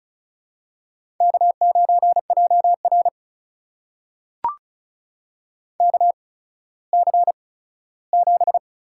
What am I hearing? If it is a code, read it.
Morse code, heard as a keyed tone, about 36 wpm, spaced as for 17 wpm, 700 Hz. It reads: K9JR KC7
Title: USA Call Signs - 8x Character Spacing - 35wpm